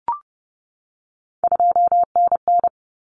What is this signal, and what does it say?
Morse code, heard as a keyed tone, about 30 wpm, 700 Hz. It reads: 2NN
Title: Common Cut Numbers - 3-Digit 30wpm